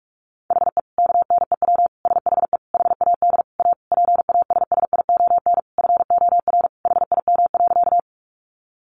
Morse code, heard as a keyed tone, about 44 wpm, 700 Hz. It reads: HE KNEW SHE HAD A PASSION FOR HIM.
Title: Sentences from Top 1300 Words 45wpm